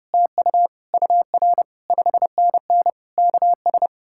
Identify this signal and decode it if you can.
Morse code, heard as a keyed tone, about 40 wpm, 700 Hz. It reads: TU UR 5NN KS